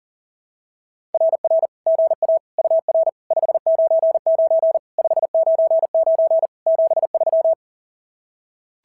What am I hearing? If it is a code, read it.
Morse code, heard as a keyed tone, about 41 wpm, 690 Hz. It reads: RR GA UR 599 599 73
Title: SOTA QSO Practice 40wpm